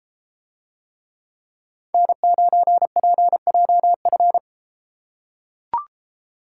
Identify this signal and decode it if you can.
Morse code, heard as a keyed tone, about 33 wpm, 700 Hz. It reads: N9PJF